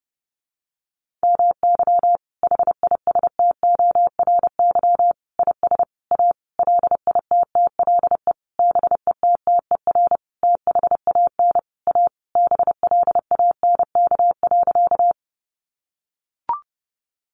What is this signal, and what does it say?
Morse code, heard as a keyed tone, about 30 wpm, 700 Hz. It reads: MY HISTORY IS A LITTLE BETTER THAN A BLANK.
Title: Sentences from Top 1500 Words 30wpm